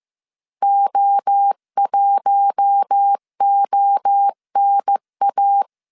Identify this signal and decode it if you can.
Morse code, heard as a keyed tone, about 15 wpm, 800 Hz. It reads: O1ONA
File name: CW / DET / 1 6 1500C S A 14wpm 2921